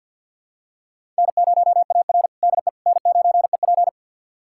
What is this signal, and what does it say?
Morse code, heard as a keyed tone, about 51 wpm, 700 Hz. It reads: N0AR DE N9EP
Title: USA Callsign DE Callsign - 4 Characters - 50wpm